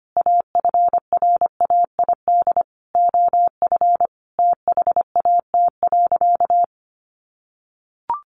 Morse code, heard as a keyed tone, about 25 wpm, 700 Hz. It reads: AFRAID OF THAT.
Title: Sentences from Top 1500 Words 25wpm